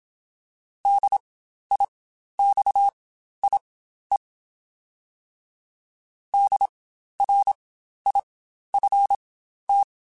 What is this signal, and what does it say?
Morse code, heard as a keyed tone, about 27 wpm, spaced as for 12 wpm, 800 Hz. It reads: DIXIE DRIFT
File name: Spaced Alphas / Spaced Alphas mp3s / Ds sp @25